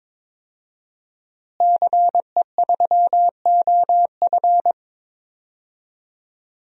Morse code, heard as a keyed tone, about 22 wpm, 700 Hz. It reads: CE3OF
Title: DXCC-100 Call Signs - 5 Characters - Repeated 2x - 22wpm